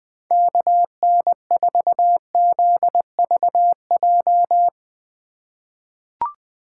Morse code, heard as a keyed tone, about 20 wpm, 700 Hz. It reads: KN4ZVJ
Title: 2023 POTA Hunter Call Signs - 20wpm